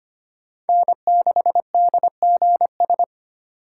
Morse code, heard as a keyed tone, about 25 wpm, 700 Hz. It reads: N6DGS
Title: USA Call Signs - 5 Characters - 3x Character Spacing - 25wpm